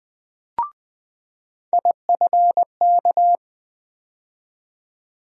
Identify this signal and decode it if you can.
Morse code, heard as a keyed tone, about 20 wpm, 700 Hz. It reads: IFK
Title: USA Call Sign Postfixes - 3 Letters 20wpm